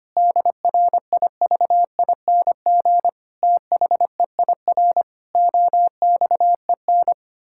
Morse code, heard as a keyed tone, about 25 wpm, 700 Hz. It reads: DRIVING THEIR OXEN
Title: Pangram Challenge v2 - 25wpm